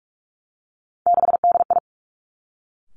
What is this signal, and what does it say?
Morse code, heard as a keyed tone, about 46 wpm, 700 Hz. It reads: 6DI
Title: Three Challenging Letter-Number Combinations 45wpm